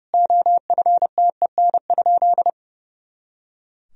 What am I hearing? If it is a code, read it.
Morse code, heard as a keyed tone, about 32 wpm, 700 Hz. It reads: OFTEN?
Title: Sentences from Top 400 Words and Abbreviations 30wpm